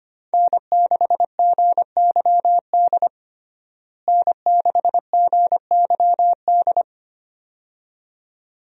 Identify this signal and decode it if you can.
Morse code, heard as a keyed tone, about 25 wpm, 700 Hz. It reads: N6GYD N6GYD
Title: USA Call Signs - 5 Characters - Repeated 2x - 25wpm